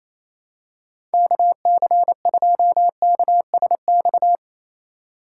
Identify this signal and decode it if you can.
Morse code, heard as a keyed tone, about 28 wpm, 700 Hz. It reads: KC2KSX
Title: USA Call Signs - 3x Character Spacing - 28wpm